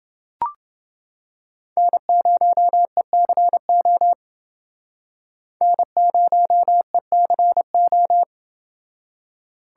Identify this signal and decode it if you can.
Morse code, heard as a keyed tone, about 30 wpm, 700 Hz. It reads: N0ECO N0ECO